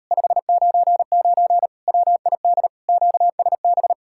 Morse code, heard as a keyed tone, about 38 wpm, 700 Hz. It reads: 599 WID QSB